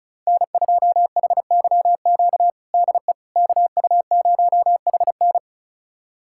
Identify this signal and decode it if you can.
Morse code, heard as a keyed tone, about 36 wpm, 700 Hz. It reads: N2HYQ DE KU0HN
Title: USA Callsign DE Callsign - 5 Characters - 35wpm